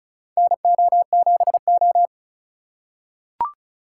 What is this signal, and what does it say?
Morse code, heard as a keyed tone, about 34 wpm, 700 Hz. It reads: NO7O
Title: USA Call Signs - 4 Characters - 3x Character Spacing - 35wpm